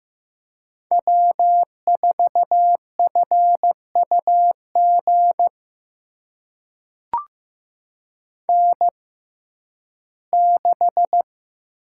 Morse code, heard as a keyed tone, about 15 wpm, spaced as for 11 wpm, 700 Hz. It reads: W4FUG N6
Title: USA Call Signs - 5 Characters - 6x Character Spacing - 15wpm